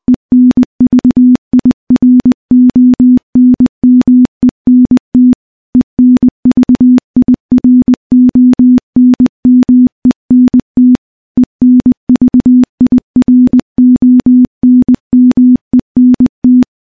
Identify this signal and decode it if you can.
Morse code, heard as a keyed tone, about 20 wpm, 250 Hz. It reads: ENVIRONMENT ENVIRONMENT ENVIRONMENT
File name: morse / morse-training-20WPM